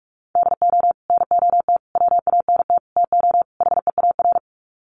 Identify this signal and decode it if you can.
Morse code, heard as a keyed tone, about 44 wpm, 700 Hz. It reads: DO NOT WANT TO HEAR
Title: Sets of 5 Words 45wpm